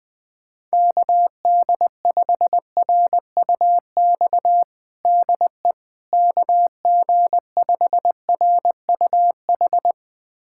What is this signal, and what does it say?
Morse code, heard as a keyed tone, about 21 wpm, 700 Hz. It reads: KD5RUX DE KG5RUH
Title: USA Callsign DE Callsign - 6 Characters - 20wpm